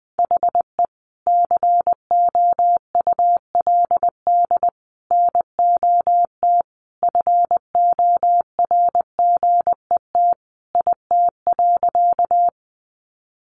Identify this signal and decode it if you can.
Morse code, heard as a keyed tone, about 20 wpm, 700 Hz. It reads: HE COULD NOT FORGET IT.
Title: Sentences from Top 1200 Words 20wpm